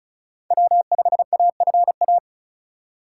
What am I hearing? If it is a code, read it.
Morse code, heard as a keyed tone, about 46 wpm, 700 Hz. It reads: W5AFA